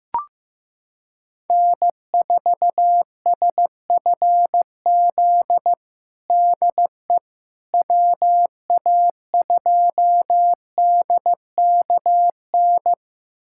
Morse code, heard as a keyed tone, about 15 wpm, 700 Hz. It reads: N4SFZ DE WA2DKN
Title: USA Callsign DE Callsign 15wpm